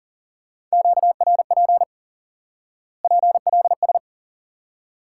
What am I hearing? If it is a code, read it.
Morse code, heard as a keyed tone, about 37 wpm, 700 Hz. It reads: QRP PLS
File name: IQE Course 3 - Lesson 89 - YAGI - Two Elements - 40wpm